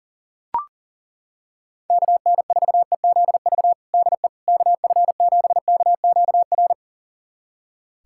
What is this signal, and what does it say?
Morse code, heard as a keyed tone, about 40 wpm, 700 Hz. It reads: KN4EZV DE KF7KQR
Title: USA Callsign DE Callsign 40wpm